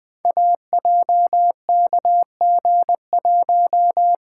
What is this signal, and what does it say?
Morse code, heard as a keyed tone, about 20 wpm, 700 Hz. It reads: AJKG1
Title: Code Groups - 5 Letter:Numbers - 20wpm